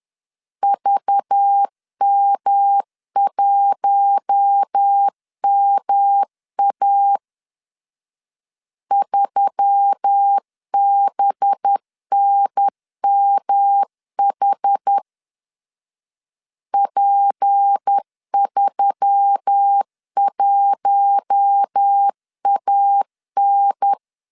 Morse code, heard as a keyed tone, about 10 wpm, 800 Hz. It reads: VM1MA 3BNMH P31AN
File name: LEZIONI / 6 PKX0 / 6 P 10wpm 2432